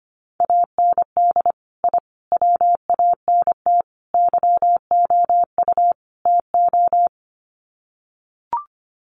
Morse code, heard as a keyed tone, about 25 wpm, 700 Hz. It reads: AND I WANT YOU TO
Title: Sets of 5 Words Encore 25wpm